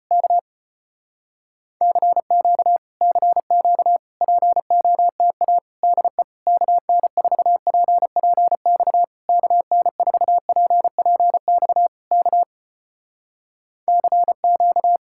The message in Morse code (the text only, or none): K CQ CQ POTA DE KN4PPX KN4PPX K CQ